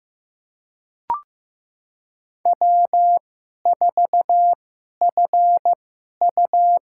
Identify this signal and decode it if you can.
Morse code, heard as a keyed tone, about 15 wpm, spaced as for 11 wpm, 700 Hz. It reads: W4FU
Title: USA Call Signs - 2x Character Spacing - 15wpm